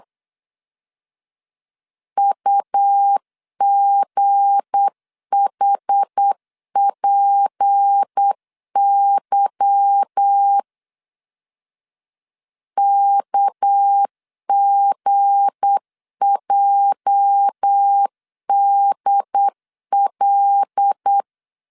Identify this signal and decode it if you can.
Morse code, heard as a keyed tone, about 8 wpm, 800 Hz. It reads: UGHPY KGJDL